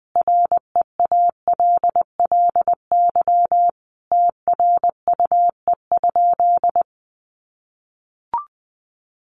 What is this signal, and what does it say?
Morse code, heard as a keyed tone, about 20 wpm, 700 Hz. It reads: REALLY TRUE?
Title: Sentences from Top 600 Words 20wpm